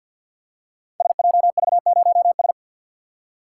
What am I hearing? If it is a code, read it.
Morse code, heard as a keyed tone, about 49 wpm, 700 Hz. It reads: SJV0S
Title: Code Groups - 5 Letter:Numbers - 50wpm